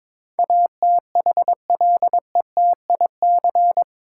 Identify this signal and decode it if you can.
Morse code, heard as a keyed tone, about 22 wpm, 700 Hz. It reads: ATHLETIC
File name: Easy Words - Speed-Racing with Hints - 22wpm